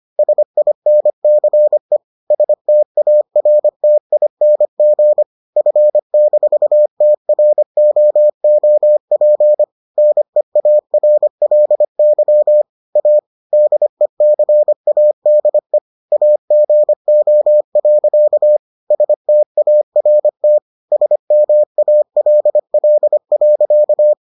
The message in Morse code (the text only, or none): SINCE STARTING F-TROOP NEARLY A DECADE AGO. START SMALL.